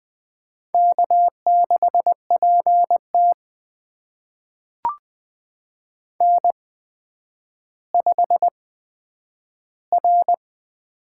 Morse code, heard as a keyed tone, about 20 wpm, spaced as for 10 wpm, 700 Hz. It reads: K6PT N5R
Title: USA Call Signs - 8x Character Spacing - 20wpm